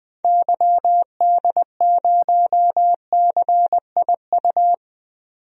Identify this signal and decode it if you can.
Morse code, heard as a keyed tone, about 20 wpm, 700 Hz. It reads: YD0CIU
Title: DXCC-100 Call Signs - 6 Characters - 20wpm